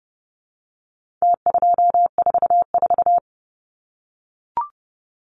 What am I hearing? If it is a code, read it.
Morse code, heard as a keyed tone, about 30 wpm, 700 Hz. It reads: T244
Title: Numbers - Contest Style - 4-Digit 30wpm